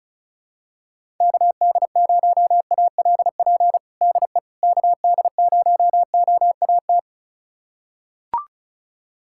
Morse code, heard as a keyed tone, about 36 wpm, 700 Hz. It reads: KD0ALP DE KD0OAT
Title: USA Callsign DE Callsign 35wpm